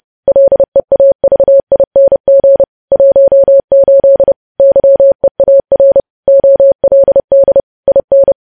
Morse code, heard as a keyed tone, about 30 wpm, 550 Hz. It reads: LEAVING 18 YEAR OLD IN
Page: News Headlines in Morse Code at 30 WPM